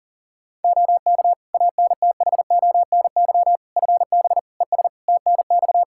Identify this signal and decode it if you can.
Morse code, heard as a keyed tone, about 40 wpm, 700 Hz. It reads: OK ANTHONY FB ES TNX
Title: Practice QSO - Exchange 2 - Info - 40wpm